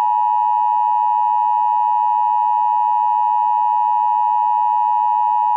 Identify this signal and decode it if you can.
Emergency Alert System (SAME) tones (the 853 and 960 Hz attention tone). It is heard from 0.3 to 5.3 s.